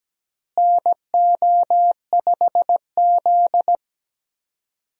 Morse code, heard as a keyed tone, about 17 wpm, 700 Hz. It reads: NO5Z
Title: USA Call Signs - 4 Characters - Repeated 3x - 17wpm